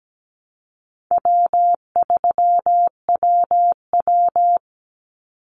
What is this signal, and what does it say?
Morse code, heard as a keyed tone, about 17 wpm, 700 Hz. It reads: W3WW
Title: Easy Call Signs Repeated 2x - 17wpm